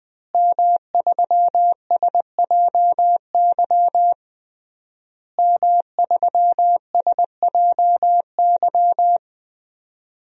Morse code, heard as a keyed tone, about 21 wpm, 700 Hz. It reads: M3SJY M3SJY
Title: DXCC-100 Call Signs - 5 Characters - Repeated 3x - 20wpm